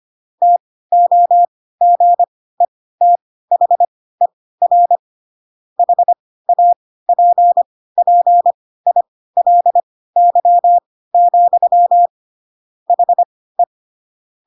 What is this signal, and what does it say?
Morse code, heard as a keyed tone, about 25 wpm, spaced as for 16 wpm, 700 Hz. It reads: TOGETHER HAPPILY, HE
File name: Pinocchio / Pinocchio ch1 10 25 ik1hge